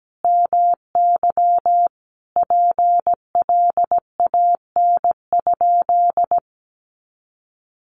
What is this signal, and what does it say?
Morse code, heard as a keyed tone, about 17 wpm, 700 Hz. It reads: MY PLAN?
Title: Sentences from Top 500 Words 17wpm